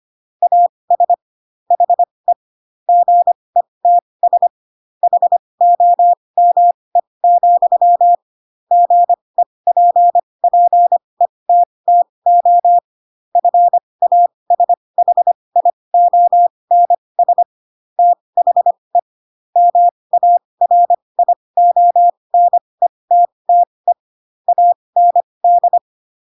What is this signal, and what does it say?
Morse code, heard as a keyed tone, about 25 wpm, spaced as for 20 wpm, 700 Hz. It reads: AS HE GETS HOME, GEPPETTO FASHIONS THE MARIONETTE AND
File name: Pinocchio / Pinocchio ch3 15 25 ik1hge